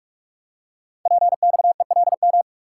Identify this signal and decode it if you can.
Morse code, heard as a keyed tone, about 44 wpm, 690 Hz. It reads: PXELM
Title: Code Groups - 5 Letters - 45wpm